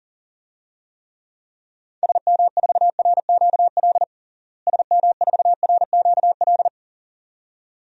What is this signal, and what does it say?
Morse code, heard as a keyed tone, about 40 wpm, 700 Hz. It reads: SM4RQL SM4RQL
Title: DXCC-100 Call Signs - 6 Characters - Repeated 2x - 40wpm